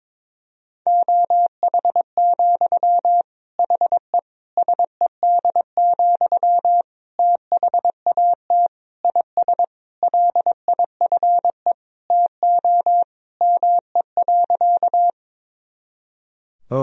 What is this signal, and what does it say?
Morse code, heard as a keyed tone, about 22 wpm, 700 Hz. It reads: OH, HE SED, THAT IS LIFE TO ME.
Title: Sentences from Top 500 Words and Abbreviations 22wpm